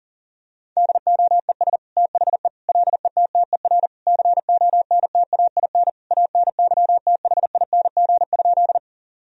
Morse code, heard as a keyed tone, about 40 wpm, 700 Hz. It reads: DOES THE LETTER CONTAIN ANYTHING?
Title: Sentences from Top 1000 Words - Review Part 2 - 40wpm